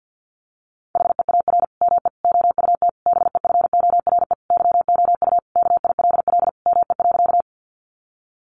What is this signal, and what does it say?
Morse code, heard as a keyed tone, about 50 wpm, 700 Hz. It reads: HEAR ME OUT BEFORE YOU KILL ME.
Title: Sentences from Top 1400 Words 50wpm